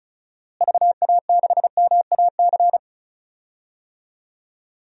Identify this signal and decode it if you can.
Morse code, heard as a keyed tone, about 36 wpm, 700 Hz. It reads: VA6MAC
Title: DXCC-100 Call Signs - 6 Characters - Repeated 3x - 35wpm